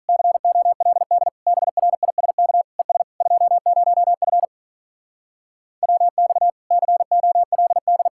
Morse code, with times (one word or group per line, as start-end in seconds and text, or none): 0.1-1.3 COLD
1.5-2.6 BRISK
2.8-3.0 ES
3.2-4.5 20F
5.8-6.5 WX
6.7-8.1 COLD